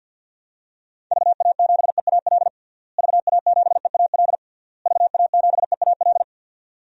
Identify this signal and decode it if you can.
Morse code, heard as a keyed tone, about 51 wpm, 700 Hz. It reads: VA7EAL VA7EAL VA7EAL
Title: DXCC-100 Call Signs - 6 Characters - Repeated 3x - 50wpm